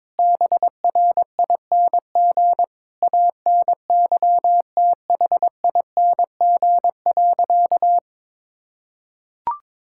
Morse code, heard as a keyed tone, about 22 wpm, 700 Hz. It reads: BRING ANYTHING.
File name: practice / 22wpm / Sentences with 4-Word Phrases - 22wpm